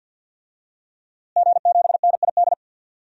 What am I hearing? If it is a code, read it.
Morse code, heard as a keyed tone, about 51 wpm, 700 Hz. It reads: G7NID
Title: DXCC-100 Call Signs - 5 Characters - Repeated 3x - 50wpm